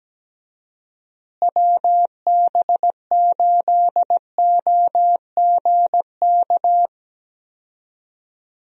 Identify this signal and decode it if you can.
Morse code, heard as a keyed tone, about 17 wpm, 700 Hz. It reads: WB8OGK
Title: USA Call Signs - 6 Characters - Repeated 3x - 17wpm